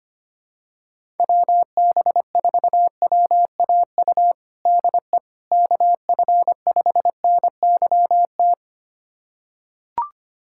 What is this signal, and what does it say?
Morse code, heard as a keyed tone, about 25 wpm, 700 Hz. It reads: WB4WAU DE KF5NYT
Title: USA Callsign DE Callsign - 6 Characters - 25wpm